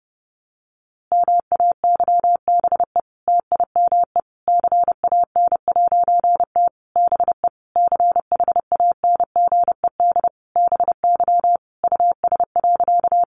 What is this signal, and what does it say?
Morse code, heard as a keyed tone, about 30 wpm, 700 Hz. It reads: MAYBE TIME CAN'T BE CHANGED BY US.
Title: A Sound Of Thunder 30wpm